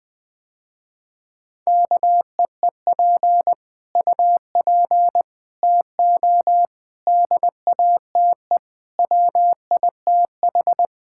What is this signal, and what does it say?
Morse code, heard as a keyed tone, about 21 wpm, 700 Hz. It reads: KEEP UP TO DATE WITH
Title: Sentences with 9-Word Phrases - Spoken Beginning - 20wpm